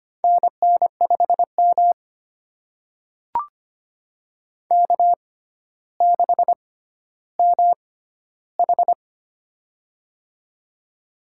Morse code, heard as a keyed tone, about 25 wpm, spaced as for 13 wpm, 700 Hz. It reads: NN5M K6MH
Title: USA Call Signs - 4 Characters - 6x Character Spacing - 25wpm